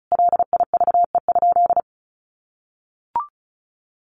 Morse code, heard as a keyed tone, about 34 wpm, 700 Hz. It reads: LIVE?
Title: Sentences from Top 1500 Words 35wpm